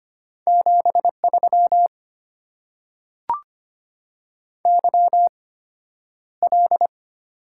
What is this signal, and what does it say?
Morse code, heard as a keyed tone, about 25 wpm, spaced as for 11 wpm, 700 Hz. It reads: 73 YL
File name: Top 90 QSO Elements - 8x Character Spacing - 25wpm